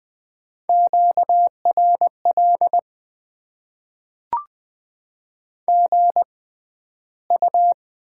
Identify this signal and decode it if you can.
Morse code, heard as a keyed tone, about 20 wpm, spaced as for 12 wpm, 700 Hz. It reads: QRL GU
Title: Top 20 QSO Elements - 6x Character Spacing - 20wpm